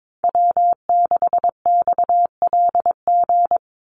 Morse code, heard as a keyed tone, about 22 wpm, 700 Hz. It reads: W6XLG
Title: USA Call Signs 22wpm